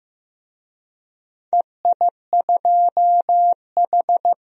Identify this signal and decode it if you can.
Morse code, heard as a keyed tone, about 15 wpm, 700 Hz. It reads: EI2H
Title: DXCC-100 Call Signs - 4 Characters - 15wpm